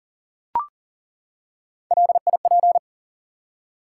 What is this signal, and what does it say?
Morse code, heard as a keyed tone, about 40 wpm, 700 Hz. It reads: LIP
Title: Three Letter Words - 40wpm